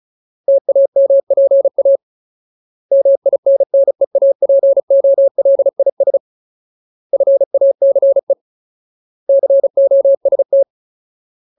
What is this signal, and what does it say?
Morse code, heard as a keyed tone, about 34 wpm, 550 Hz. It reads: TAMPA MINNEAPOLIS FACE COST
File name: Ultimate Word List / 35WPM Ultimate